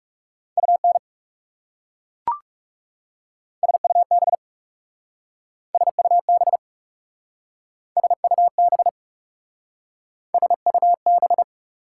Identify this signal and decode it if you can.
Morse code, heard as a keyed tone, about 39 wpm, 700 Hz. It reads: UN SUB SUB SUB SUB